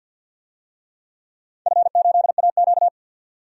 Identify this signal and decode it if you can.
Morse code, heard as a keyed tone, about 53 wpm, 700 Hz. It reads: F8AQ
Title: DXCC-100 Call Signs - 4 Characters - 50wpm